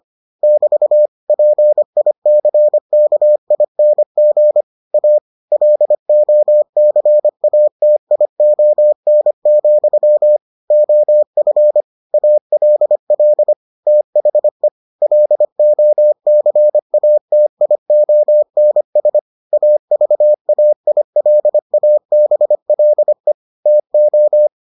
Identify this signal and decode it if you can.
Morse code, heard as a keyed tone, about 25 wpm, 600 Hz. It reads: <BT> PICKING A LOCATION, OF ALL THE LOCATIONS AVAILABLE TO